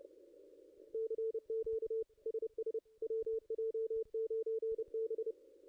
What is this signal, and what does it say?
Morse code, heard as a keyed tone, about 30 wpm, 450 Hz, with radio noise. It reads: CQ SS WJ9B
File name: contests / ss-cw-k5tr-op-n5rz -0400